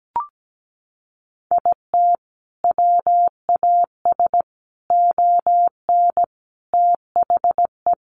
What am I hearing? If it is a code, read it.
Morse code, heard as a keyed tone, about 17 wpm, 700 Hz. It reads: IT WAS ON THE
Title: Sentences from Top 1200 Words 17wpm